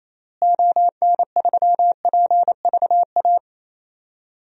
Morse code, heard as a keyed tone, about 28 wpm, 700 Hz. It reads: ON3PVA
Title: DXCC-100 Call Signs - 6 Characters - Repeated 2x - 28wpm